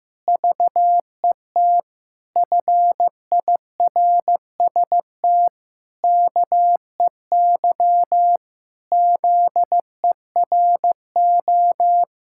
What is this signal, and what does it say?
Morse code, heard as a keyed tone, about 15 wpm, 700 Hz. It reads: VET FIRST KEY ZERO
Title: Pangram Challenge v2 - 15wpm